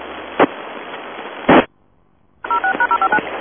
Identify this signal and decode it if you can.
DTMF, heard as telephone touch tones, tones of about 80 ms. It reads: 06#059
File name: uploader